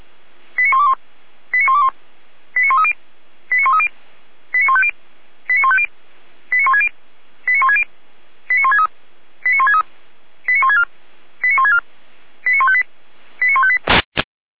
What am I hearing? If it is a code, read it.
Five-tone selective call (ZVEI), heard as quick runs of five tones, tones of about 80 ms. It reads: 89121 89121 89130 89130 89150 89150 89160 89160 89163 89163 89164 89164 89158 89158